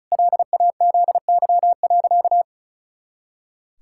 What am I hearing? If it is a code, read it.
Morse code, heard as a keyed tone, about 34 wpm, 700 Hz. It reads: LAZY.
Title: Pangram Challenge v2 - 35wpm